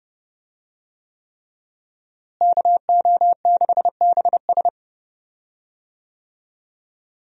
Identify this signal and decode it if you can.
Morse code, heard as a keyed tone, about 30 wpm, 700 Hz. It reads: KO6BS
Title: USA Call Signs - 5 Characters - Repeated 3x - 30wpm